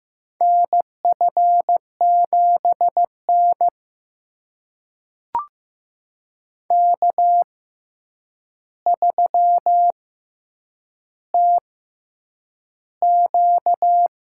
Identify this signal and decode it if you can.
Morse code, heard as a keyed tone, about 15 wpm, spaced as for 8 wpm, 700 Hz. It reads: NF7N K3TQ
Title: USA Call Signs - 4 Characters - 6x Character Spacing - 15wpm